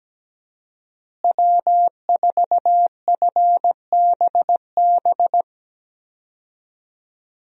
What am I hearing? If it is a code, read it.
Morse code, heard as a keyed tone, about 17 wpm, 700 Hz. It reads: W4FBB
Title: USA Call Signs - 5 Characters - Repeated 2x - 17wpm